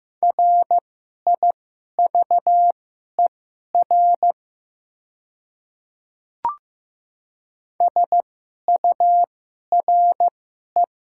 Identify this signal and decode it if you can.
Morse code, heard as a keyed tone, about 15 wpm, spaced as for 11 wpm, 700 Hz. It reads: RIVER SURE
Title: 2x Character Spacing - Top 300 Words - 15wpm